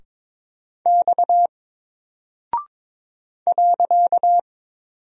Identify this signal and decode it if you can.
Morse code, heard as a keyed tone, about 22 wpm, 700 Hz. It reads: X .
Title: Single Character (Letter:Number:Punctuation:Prosign) Alt 22wpm